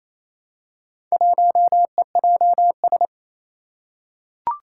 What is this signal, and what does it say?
Morse code, heard as a keyed tone, about 29 wpm, 700 Hz. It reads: 1EJS
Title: Four Challenging Letter-Number Combinations 28wpm